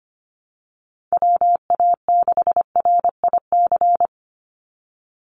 Morse code, heard as a keyed tone, about 25 wpm, 700 Hz. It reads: WA6RIC
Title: NA SOTA Chaser Callsigns 25wpm